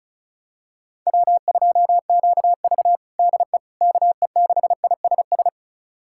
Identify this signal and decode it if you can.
Morse code, heard as a keyed tone, about 34 wpm, 700 Hz. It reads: W2QV DE KE6ISS
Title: USA Callsign DE Callsign 35wpm